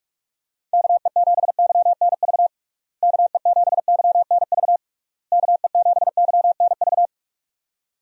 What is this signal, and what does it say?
Morse code, heard as a keyed tone, about 46 wpm, 700 Hz. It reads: KE7YNV KE7YNV KE7YNV
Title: USA Call Signs - 6 Characters - Repeated 3x - 45wpm